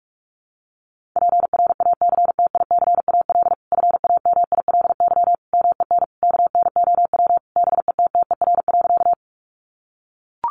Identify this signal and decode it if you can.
Morse code, heard as a keyed tone, about 44 wpm, 700 Hz. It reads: PRACTICAL FAMILY MEN KNOW BETTER.
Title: Sentences from Top 1400 Words 45wpm